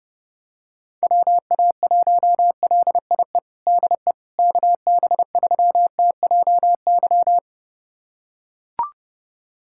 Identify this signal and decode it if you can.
Morse code, heard as a keyed tone, about 32 wpm, 700 Hz. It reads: WA1LIE DE KB3TJY
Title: USA Callsign DE Callsign - 6 Characters - 30wpm